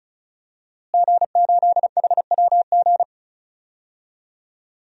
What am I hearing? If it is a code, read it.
Morse code, heard as a keyed tone, about 36 wpm, 700 Hz. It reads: G8HWG
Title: DXCC-100 Call Signs - 5 Characters - Repeated 3x - 35wpm